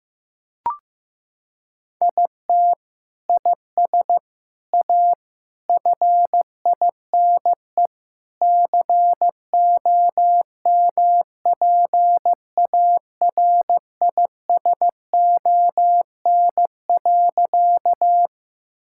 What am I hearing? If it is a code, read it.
Morse code, heard as a keyed tone, about 15 wpm, 700 Hz. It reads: IT IS A FINE COMPARISON.
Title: Sentences from Top 1600 Words 15wpm